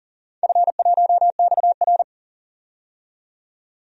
Morse code, heard as a keyed tone, about 40 wpm, 700 Hz. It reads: F1XR
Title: DXCC-100 Call Signs - 4 Characters - 40wpm